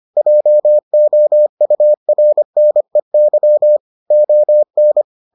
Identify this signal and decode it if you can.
Morse code, heard as a keyed tone, about 25 wpm, 600 Hz. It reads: JOURNEY ON